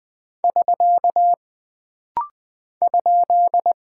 Morse code, heard as a keyed tone, about 20 wpm, 700 Hz. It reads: <SK> ?
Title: Single Punctuation-Prosign - Rapid-Fire - 20wpm